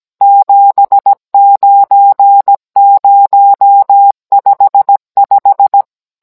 Morse code, heard as a keyed tone, about 17 wpm, 800 Hz. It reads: 79055